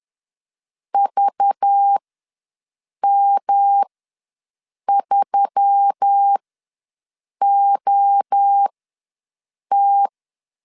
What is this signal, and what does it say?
Morse code, heard as a keyed tone, about 10 wpm, spaced as for 6 wpm, 800 Hz. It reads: VM3OT
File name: LEZIONI / 5 LBVC / 5 LBV ws 10wpm 3645